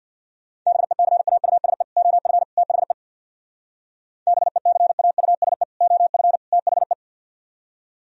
Morse code, heard as a keyed tone, about 62 wpm, 700 Hz. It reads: BECAUSE OF THE BECAUSE OF THE
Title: Speed-Racing - Sets of 3 Words 50wpm